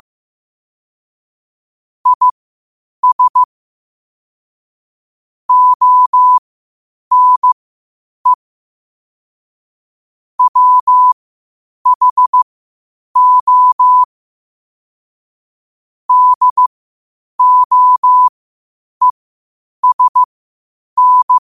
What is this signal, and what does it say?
Morse code, heard as a keyed tone, about 15 wpm, spaced as for 8 wpm, 1000 Hz. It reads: IS ONE WHO DOESN
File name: qotdfiles / 8wpm